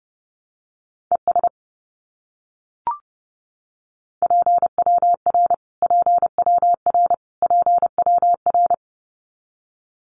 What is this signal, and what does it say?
Morse code, heard as a keyed tone, about 30 wpm, 700 Hz. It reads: ES PWR PWR PWR
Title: Abbreviations Repeated 3x - 30wpm